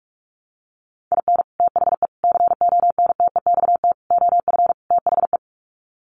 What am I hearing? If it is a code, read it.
Morse code, heard as a keyed tone, about 44 wpm, 700 Hz. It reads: IN THE CONTEXT OF THE
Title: Sets of 5 Words 45wpm